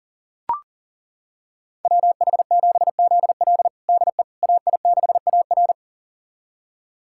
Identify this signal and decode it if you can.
Morse code, heard as a keyed tone, about 40 wpm, 700 Hz. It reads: WH7ZL DE AI6AR
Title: USA Callsign DE Callsign - 5 Characters - 40wpm